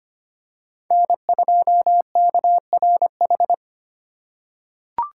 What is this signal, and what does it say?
Morse code, heard as a keyed tone, about 25 wpm, 700 Hz. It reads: N2KRH
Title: USA Call Signs - 5 Characters - 3x Character Spacing - 25wpm